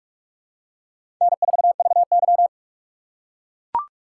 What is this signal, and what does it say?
Morse code, heard as a keyed tone, about 44 wpm, 700 Hz. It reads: N4VY
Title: USA Call Signs - 6x Character Spacing - 45wpm